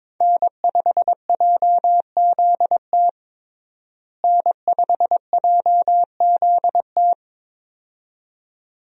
Morse code, heard as a keyed tone, about 22 wpm, 700 Hz. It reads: N5JZT N5JZT
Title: USA Call Signs - 5 Characters - Repeated 2x - 22wpm